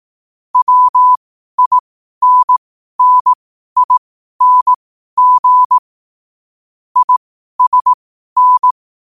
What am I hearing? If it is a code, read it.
Morse code, heard as a keyed tone, about 18 wpm, spaced as for 12 wpm, 1000 Hz. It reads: WINNING ISN
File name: qotdfiles / 12wpm